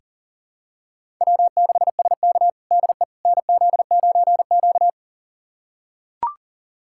Morse code, heard as a keyed tone, about 37 wpm, 700 Hz. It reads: W6SK DE NZ9Q
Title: USA Callsign DE Callsign - 4 Characters - 40wpm